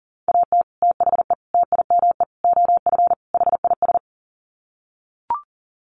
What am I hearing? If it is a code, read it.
Morse code, heard as a keyed tone, about 40 wpm, 700 Hz. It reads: AT THE TIME OF HIS
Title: Sets of 5 Words Encore2 40wpm